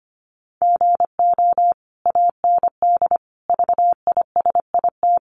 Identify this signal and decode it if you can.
Morse code, heard as a keyed tone, about 25 wpm, 700 Hz. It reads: GO AND VISIT
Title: Sentences from Top 1100 Words 25wpm